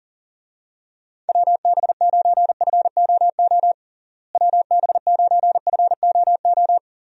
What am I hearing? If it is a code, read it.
Morse code, heard as a keyed tone, about 40 wpm, 700 Hz. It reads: WB9FOO WB9FOO
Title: USA Call Signs - 6 Characters - Repeated 3x - 40wpm